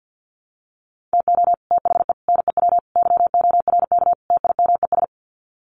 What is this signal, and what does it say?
Morse code, heard as a keyed tone, about 50 wpm, 700 Hz. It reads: TO THE NEW YORK TIMES
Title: Sets of 5 Words Encore2 50wpm